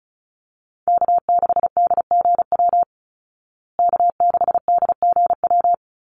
Morse code, heard as a keyed tone, about 34 wpm, 700 Hz. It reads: K6DGW K6DGW
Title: Call Signs Repeated 3x - 35wpm